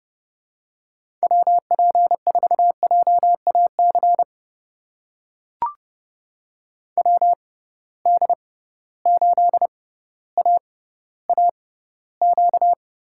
Morse code, heard as a keyed tone, about 30 wpm, spaced as for 15 wpm, 700 Hz. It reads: WP4JAC WD8AAQ
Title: USA Call Signs - 6 Characters - 6x Character Spacing - 30wpm